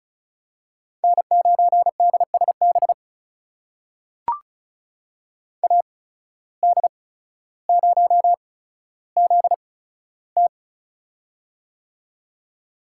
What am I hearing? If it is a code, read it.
Morse code, heard as a keyed tone, about 34 wpm, spaced as for 15 wpm, 700 Hz. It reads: N9DSB AD0ZT
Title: USA Call Signs - 5 Characters - 8x Character Spacing - 35wpm